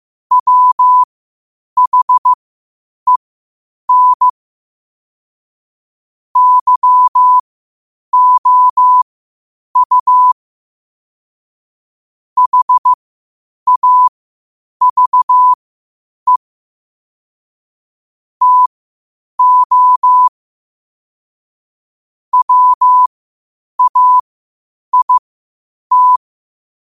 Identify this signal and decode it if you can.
Morse code, heard as a keyed tone, about 15 wpm, spaced as for 8 wpm, 1000 Hz. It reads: WHEN YOU HAVE TO WAIT
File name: qotdfiles / 8wpm